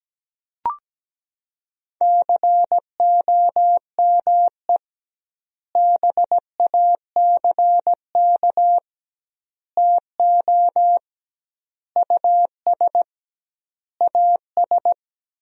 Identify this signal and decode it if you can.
Morse code, heard as a keyed tone, about 17 wpm, 700 Hz. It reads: COME BACK TO US AS
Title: 2x Word-Spacing - Sentences from Top 500 Words - 17wpm